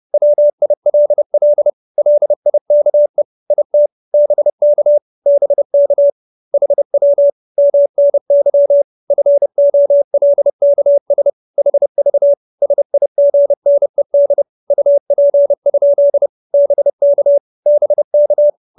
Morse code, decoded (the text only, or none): WILL LIKE IT BK BK HW MNY FOLKS HV SIGNED UP? BK BK